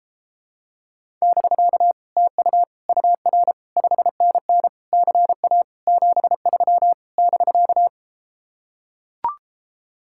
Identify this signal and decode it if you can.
Morse code, heard as a keyed tone, about 33 wpm, 700 Hz. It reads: <BK> TU UR 5NN CA 73 <BK>